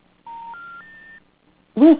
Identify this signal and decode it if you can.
Special information tone.